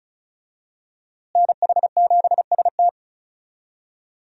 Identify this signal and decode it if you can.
Morse code, heard as a keyed tone, about 36 wpm, 700 Hz. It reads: NH7ST
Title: USA Call Signs - 5 Characters - Repeated 3x - 35wpm